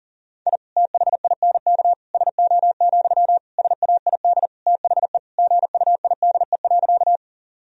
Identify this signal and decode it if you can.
Morse code, heard as a keyed tone, about 40 wpm, 700 Hz. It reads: I THINK SO, SAID THE GUIDE.
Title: Sentences from Top 1000 Words - Review Part 3 - 40wpm